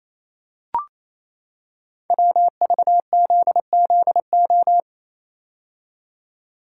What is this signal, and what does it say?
Morse code, heard as a keyed tone, about 28 wpm, 700 Hz. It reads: WVZZO
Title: Code Groups - 5 Letters - 28wpm